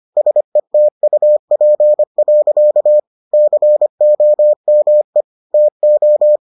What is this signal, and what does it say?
Morse code, heard as a keyed tone, about 25 wpm, 600 Hz. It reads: SETUP. COME TO